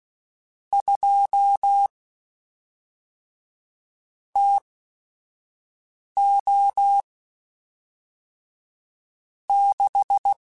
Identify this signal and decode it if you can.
Morse code, heard as a keyed tone, about 16 wpm, spaced as for 5 wpm, 800 Hz. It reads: 2 TO 6
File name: Weather Sets / Weather Sets-mp3s / Weather 4 10 @5